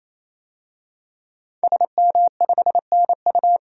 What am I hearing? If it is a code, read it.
Morse code, heard as a keyed tone, about 28 wpm, 700 Hz. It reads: SM5NU
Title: DXCC-100 Call Signs - 5 Characters - 28wpm